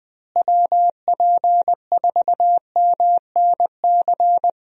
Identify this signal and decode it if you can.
Morse code, heard as a keyed tone, about 20 wpm, 700 Hz. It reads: WP4MNC
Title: USA Call Signs - 6 Characters - Repeated 2x - 20wpm